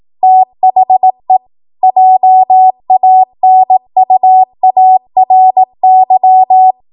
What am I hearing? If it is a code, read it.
Morse code, heard as a keyed tone, about 18 wpm, 750 Hz. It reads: THE JANUARY
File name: Archive / 18 WPM / 18WPM